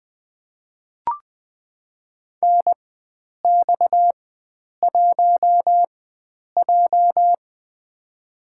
Morse code, heard as a keyed tone, about 20 wpm, spaced as for 9 wpm, 700 Hz. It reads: NX1J
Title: USA Call Signs - 4 Characters - 4x Character Spacing - 20wpm